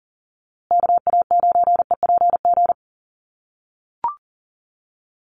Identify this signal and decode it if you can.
Morse code, heard as a keyed tone, about 39 wpm, 700 Hz. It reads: KA9EPG
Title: USA Call Signs Repeated 3x 40wpm